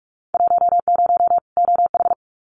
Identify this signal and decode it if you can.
Morse code, heard as a keyed tone, about 46 wpm, 700 Hz. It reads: 10 OH
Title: Winter Field Day Exchanges 45wpm